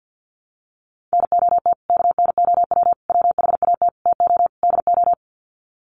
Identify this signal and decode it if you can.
Morse code, heard as a keyed tone, about 48 wpm, 700 Hz. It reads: NOT KNOW WHAT TO DO
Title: Sets of 5 Words 50wpm